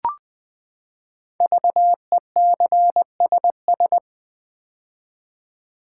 Morse code, heard as a keyed tone, about 20 wpm, 700 Hz. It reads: VECSS
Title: Code Groups - 5 Letters - 20wpm